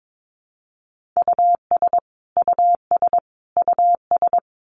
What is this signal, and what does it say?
Morse code, heard as a keyed tone, about 22 wpm, 700 Hz. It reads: US US US
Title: Top 200 Words Repeated 22wpm